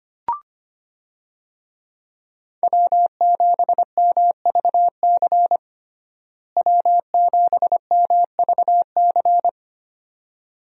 Morse code, heard as a keyed tone, about 25 wpm, 700 Hz. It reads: W7MVC W7MVC
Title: USA Call Signs - 5 Characters - Repeated 2x - 25wpm